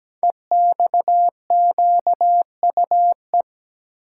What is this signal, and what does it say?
Morse code, heard as a keyed tone, about 17 wpm, 700 Hz. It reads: EXQUE
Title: Code Groups - 5 Letter:Numbers - 17wpm